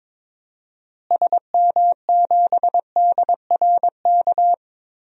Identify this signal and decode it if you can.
Morse code, heard as a keyed tone, about 22 wpm, 700 Hz. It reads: SM7DRK
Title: DXCC-100 Call Signs - 6 Characters - Repeated 2x - 22wpm